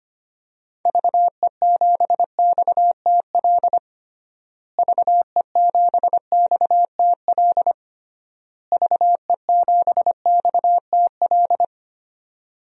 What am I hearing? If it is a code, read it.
Morse code, heard as a keyed tone, about 25 wpm, 700 Hz. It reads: VE7XTL VE7XTL VE7XTL
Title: DXCC-100 Call Signs - 6 Characters - Repeated 3x - 25wpm